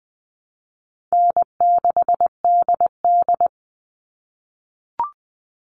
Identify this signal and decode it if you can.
Morse code, heard as a keyed tone, about 20 wpm, 700 Hz. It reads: N6DD
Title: All Cut Numbers - 4-Digit 20wpm